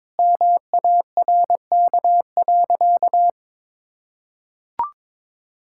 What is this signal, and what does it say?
Morse code, heard as a keyed tone, about 21 wpm, 700 Hz. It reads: MARK.
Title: Sentences from Top 1000 Words - Review Part 6 - 22wpm